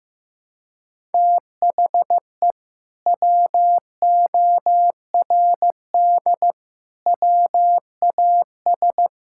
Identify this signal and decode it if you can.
Morse code, heard as a keyed tone, about 16 wpm, 700 Hz. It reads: THE WORD WAS
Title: Sentences from Top 1000 Words - Review Part 5 - 15wpm